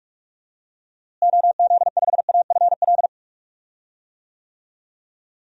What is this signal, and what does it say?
Morse code, heard as a keyed tone, about 46 wpm, 700 Hz. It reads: OZ5AFL
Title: DXCC-100 Call Signs - 6 Characters - Repeated 3x - 45wpm